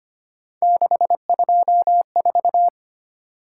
Morse code, heard as a keyed tone, about 25 wpm, 700 Hz. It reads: 624
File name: Three Letter-Number Combinations - 25wpm